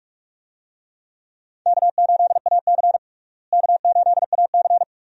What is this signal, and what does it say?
Morse code, heard as a keyed tone, about 42 wpm, 700 Hz. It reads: K8AC K8AC
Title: USA Call Signs - 4 Characters - Repeated 2x - 45wpm